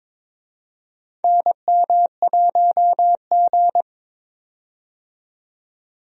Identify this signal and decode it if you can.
Morse code, heard as a keyed tone, about 22 wpm, 700 Hz. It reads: NM1G
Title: USA Call Signs - 4 Characters - 22wpm